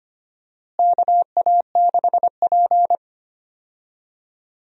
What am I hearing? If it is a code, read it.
Morse code, heard as a keyed tone, about 25 wpm, 700 Hz. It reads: KA6P
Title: USA Call Signs - 4 Characters - Repeated 2x - 25wpm